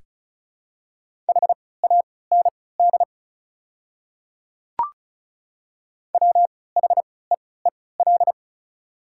Morse code, heard as a keyed tone, about 36 wpm, spaced as for 20 wpm, 700 Hz. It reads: HAND WHEEL
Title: 3x Character Spacing - Top 500 Words - 35wpm